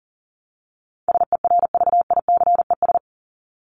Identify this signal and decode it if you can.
Morse code, heard as a keyed tone, about 40 wpm, 700 Hz. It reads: SERVICES
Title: ES:S Suffix - Top 100 Words with Base Words 40wpm